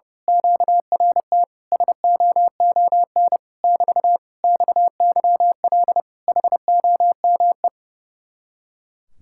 Morse code, heard as a keyed tone, about 29 wpm, 700 Hz. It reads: QRT SOON <BT> XYL HOME
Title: Speed-Racing Common QSO Phrases - 30wpm